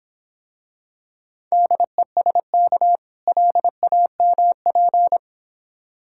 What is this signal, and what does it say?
Morse code, heard as a keyed tone, about 26 wpm, 700 Hz. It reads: DESK LAMP